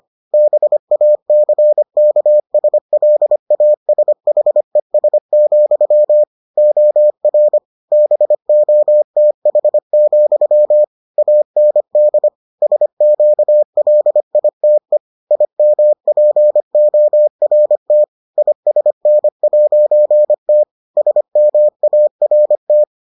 Morse code, heard as a keyed tone, about 25 wpm, 600 Hz. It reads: BACKSLASHES, OR BOTH, AND SQLITE IMPORT ISN'T SMART